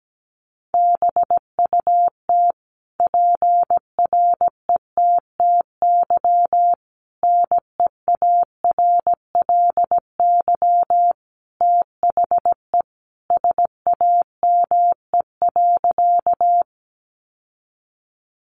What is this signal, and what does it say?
Morse code, heard as a keyed tone, about 17 wpm, 700 Hz. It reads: BUT PRETTY NEARLY THE SAME.
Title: Sentences from Top 1300 Words 17wpm